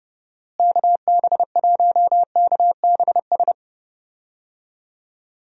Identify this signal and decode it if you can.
Morse code, heard as a keyed tone, about 30 wpm, 700 Hz. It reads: KB1KBS
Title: USA Call Signs - 6 Characters - Repeated 3x - 30wpm